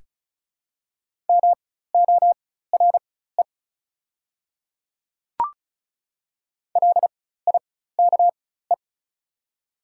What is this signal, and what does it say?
Morse code, heard as a keyed tone, about 34 wpm, spaced as for 16 wpm, 700 Hz. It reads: MORE LIKE
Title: 4x Character Spacing - Top 100 Words - 35wpm